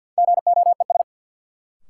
Morse code, heard as a keyed tone, about 51 wpm, 700 Hz. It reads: GOES
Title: Sentences from Top 1000 Words - Review Part 5 - New Word Drill - 50wpm